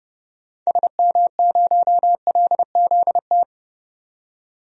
Morse code, heard as a keyed tone, about 30 wpm, 700 Hz. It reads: SM0LZT
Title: DXCC-100 Call Signs - 6 Characters - 30wpm